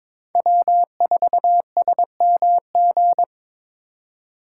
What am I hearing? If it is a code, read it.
Morse code, heard as a keyed tone, about 22 wpm, 700 Hz. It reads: W4SMG
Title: USA Call Signs - 5 Characters - 4x Character Spacing - 22wpm